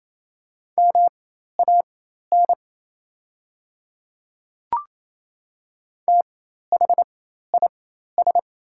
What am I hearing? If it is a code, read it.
Morse code, heard as a keyed tone, about 28 wpm, spaced as for 13 wpm, 700 Hz. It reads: MAN THIS
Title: 4x Character Spacing - Top 300 Words - 28wpm